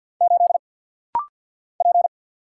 Morse code, heard as a keyed tone, about 50 wpm, 700 Hz. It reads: 8 P
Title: Single Letter-Number Alt - Rapid-Fire 50wpm